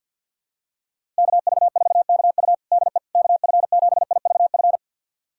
Morse code, heard as a keyed tone, about 48 wpm, 700 Hz. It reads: KV4KU DE KF7IVF
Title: USA Callsign DE Callsign 50wpm